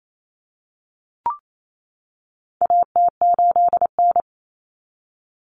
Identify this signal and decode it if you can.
Morse code, heard as a keyed tone, about 28 wpm, 700 Hz. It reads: AT8N
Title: Common Cut Numbers - 4-Digit 28wpm